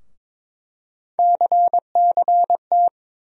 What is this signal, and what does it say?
Morse code, heard as a keyed tone, about 22 wpm, 700 Hz. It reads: CCT
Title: USA Call Sign Postfixes - 3 Letters 22wpm